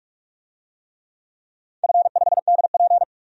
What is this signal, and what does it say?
Morse code, heard as a keyed tone, about 46 wpm, 700 Hz. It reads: F5DP
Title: DXCC-100 Call Signs - 4 Characters - 45wpm